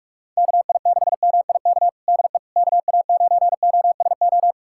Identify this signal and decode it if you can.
Morse code, heard as a keyed tone, about 46 wpm, 700 Hz. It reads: KI6MIK DE KA9OSO
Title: USA Callsign DE Callsign - 6 Characters - 45wpm